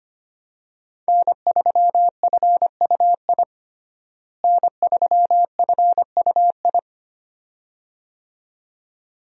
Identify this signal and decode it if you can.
Morse code, heard as a keyed tone, about 25 wpm, 700 Hz. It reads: N3FUI N3FUI
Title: USA Call Signs - 5 Characters - Repeated 2x - 25wpm